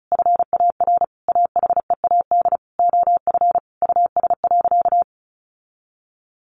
Morse code, heard as a keyed tone, about 34 wpm, 700 Hz. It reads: FAR AHEAD OF US.
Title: Sentences from Top 1100 Words 35wpm